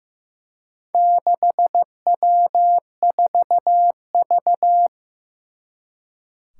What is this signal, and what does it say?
Morse code, heard as a keyed tone, about 15 wpm, 700 Hz. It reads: 6W4V
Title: Four Challenging Letter-Number Combinations 15wpm